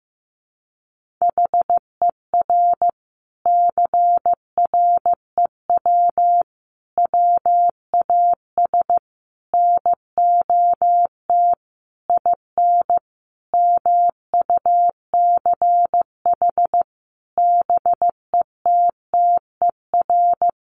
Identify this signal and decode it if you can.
Morse code, heard as a keyed tone, about 15 wpm, 700 Hz. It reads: HER CREW WAS NOT IN MUCH BETTER
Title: Sentences from Top 1500 Words 15wpm